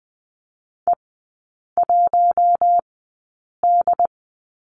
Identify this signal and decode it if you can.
Morse code, heard as a keyed tone, about 20 wpm, spaced as for 8 wpm, 700 Hz. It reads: E1D
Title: 067 Lesson 12e - Letter d - Three Characters 20wpm